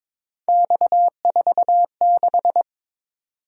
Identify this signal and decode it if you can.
Morse code, heard as a keyed tone, about 22 wpm, 700 Hz. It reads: X46
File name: practice / 22wpm / Three Letter-Number Combinations - 22wpm